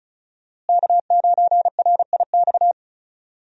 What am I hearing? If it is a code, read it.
Morse code, heard as a keyed tone, about 36 wpm, 700 Hz. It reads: K9RIX
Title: USA Call Signs - 5 Characters - Repeated 2x - 35wpm